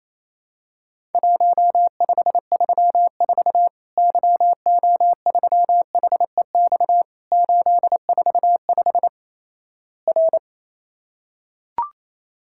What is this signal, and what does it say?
Morse code, heard as a keyed tone, about 28 wpm, 700 Hz. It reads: 1534 YO3HEX 845 R
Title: Worked All Europe DX Contest - QTC Exchange - 28wpm